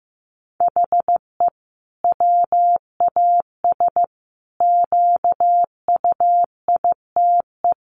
Morse code, heard as a keyed tone, about 15 wpm, 700 Hz. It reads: HE WAS QUITE
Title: Sentences from Top 1400 Words 15wpm